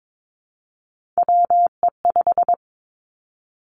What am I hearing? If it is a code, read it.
Morse code, heard as a keyed tone, about 22 wpm, 700 Hz. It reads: WE5
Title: Three Challenging Letter-Number Combinations 22wpm